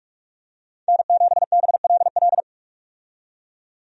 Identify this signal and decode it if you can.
Morse code, heard as a keyed tone, about 42 wpm, 700 Hz. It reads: N7BLL
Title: USA Call Signs - 5 Characters - 45wpm